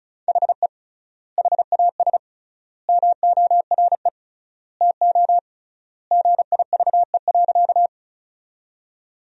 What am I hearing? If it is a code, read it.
Morse code, heard as a keyed tone, about 36 wpm, 700 Hz. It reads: HE HAS MORE TO GIVE.
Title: 3x Word-Spacing - Sentences from Top 200 Words - 35wpm